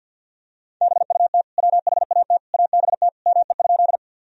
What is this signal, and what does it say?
Morse code, heard as a keyed tone, about 50 wpm, 700 Hz. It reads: BUT WHAT ABT ME?
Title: Sentences from Top 200 Words and Abbreviations 50wpm